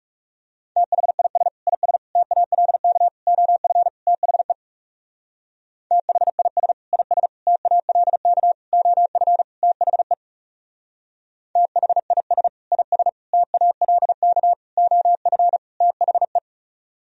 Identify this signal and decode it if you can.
Morse code, heard as a keyed tone, about 35 wpm, 700 Hz. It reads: THIS IS TALK OF THE THIS IS TALK OF THE THIS IS TALK OF THE
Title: Speed-Racing - Sets of 5 Words 30wpm